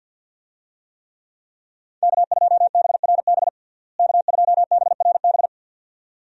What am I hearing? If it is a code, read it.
Morse code, heard as a keyed tone, about 47 wpm, 700 Hz. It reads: K2BRB K2BRB
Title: USA Call Signs - 5 Characters - Repeated 2x - 50wpm